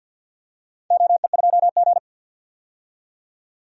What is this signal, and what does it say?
Morse code, heard as a keyed tone, about 47 wpm, 700 Hz. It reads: OE2G
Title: DXCC-100 Call Signs - 4 Characters - 50wpm